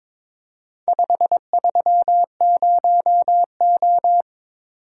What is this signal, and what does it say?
Morse code, heard as a keyed tone, about 22 wpm, 700 Hz. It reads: 530O
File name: practice / 22wpm / Four Letter-Number Combinations - 22wpm